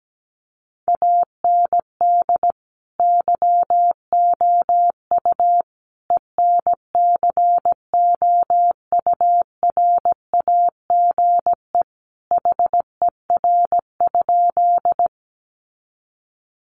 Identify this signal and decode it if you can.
Morse code, heard as a keyed tone, about 17 wpm, 700 Hz. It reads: AND YOU ENCOURAGE HER?
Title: Sentences from Top 1400 Words 17wpm